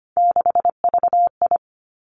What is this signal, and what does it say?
Morse code, heard as a keyed tone, about 25 wpm, 700 Hz. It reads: 6VI
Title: Three Challenging Letter-Number Combinations 25wpm